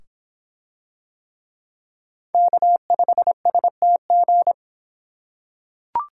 Morse code, heard as a keyed tone, about 24 wpm, 700 Hz. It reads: K5STG